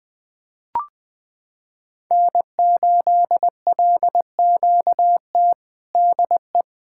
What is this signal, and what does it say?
Morse code, heard as a keyed tone, about 20 wpm, 700 Hz. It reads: N8LQT DE
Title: Practice QSO - Exchange 4 - Ending - 20wpm